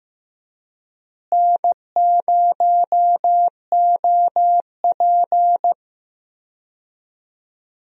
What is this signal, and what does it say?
Morse code, heard as a keyed tone, about 15 wpm, 700 Hz. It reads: N0OP
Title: USA Call Signs - 4 Characters - Repeated 2x - 15wpm